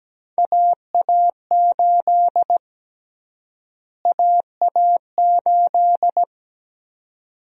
Morse code, heard as a keyed tone, about 17 wpm, 700 Hz. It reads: AA8 AA8
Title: USA Call Sign Prefixes - Repeated 2x - 17wpm